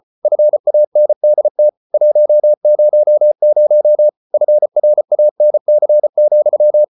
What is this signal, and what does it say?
Morse code, heard as a keyed tone, about 34 wpm, 600 Hz. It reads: FANDT 100 FRANC,
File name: Ferie QRQ 34wpm